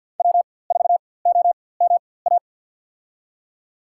Morse code, heard as a keyed tone, about 47 wpm, spaced as for 23 wpm, 700 Hz. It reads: W4OMA
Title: USA Call Signs - 5 Characters - 4x Character Spacing - 50wpm